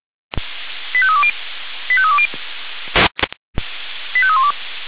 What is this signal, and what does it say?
Five-tone selective call (ZVEI), heard as quick runs of five tones, tones of about 70 ms. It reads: 85320 85320 85312